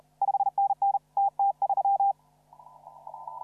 Morse code, heard as a keyed tone, about 32 wpm, 800 Hz. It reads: 5NN TT3
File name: CW / RHOB CW 2017 HA6P 0001